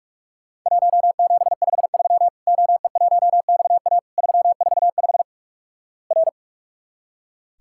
Morse code, heard as a keyed tone, about 46 wpm, 700 Hz. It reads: 1753 OE1XA 345 R
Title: Worked All Europe DX Contest - QTC Exchange - 45wpm